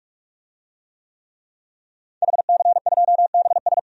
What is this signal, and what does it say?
Morse code, heard as a keyed tone, about 46 wpm, 700 Hz. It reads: HC2BS
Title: DXCC-100 Call Signs - 5 Characters - 45wpm